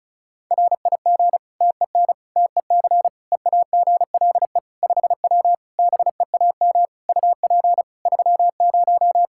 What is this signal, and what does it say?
Morse code, heard as a keyed tone, about 36 wpm, 700 Hz. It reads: RIG TEN TEC EAGLE 5W BEAM UP 30
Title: Practice QSO - Exchange 2 - Info - 35wpm